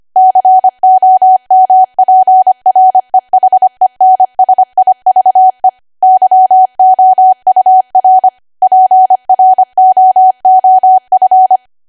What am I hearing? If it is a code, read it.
Morse code, heard as a keyed tone, about 25 wpm, 750 Hz. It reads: COMPREHENSIVE YOUR PROOF
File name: Archive / 25 WPM / 25WPM